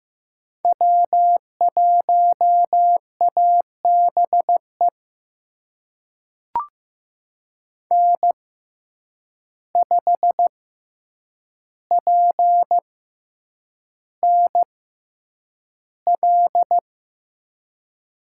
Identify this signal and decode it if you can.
Morse code, heard as a keyed tone, about 15 wpm, spaced as for 8 wpm, 700 Hz. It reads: W1ABE N5PNL
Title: USA Call Signs - 5 Characters - 6x Character Spacing - 15wpm